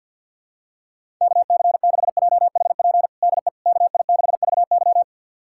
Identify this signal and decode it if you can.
Morse code, heard as a keyed tone, about 47 wpm, 700 Hz. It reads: KC6JHP DE KI6VY
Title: USA Callsign DE Callsign 50wpm